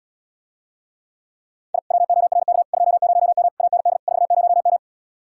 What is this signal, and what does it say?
Morse code, heard as a keyed tone, about 77 wpm, 700 Hz. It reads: I WORK 20M AND 40M